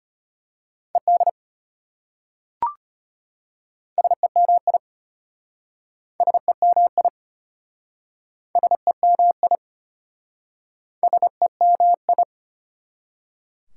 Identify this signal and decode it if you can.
Morse code, heard as a keyed tone, about 33 wpm, 690 Hz. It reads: ED SEMI SEMI SEMI SEMI